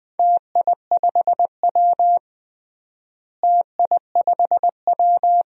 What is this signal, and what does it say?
Morse code, heard as a keyed tone, about 20 wpm, 700 Hz. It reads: TI5W TI5W
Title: DXCC-100 Call Signs - 4 Characters - Repeated 2x - 20wpm